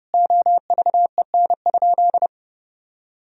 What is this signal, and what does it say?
Morse code, heard as a keyed tone, about 30 wpm, 700 Hz. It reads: OVEN?
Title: Sentences from Top 1000 Words - Review Part 4 - 30wpm